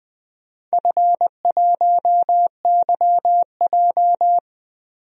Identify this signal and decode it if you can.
Morse code, heard as a keyed tone, about 20 wpm, 700 Hz. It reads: F1YJ
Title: DXCC-100 Call Signs - 4 Characters - 20wpm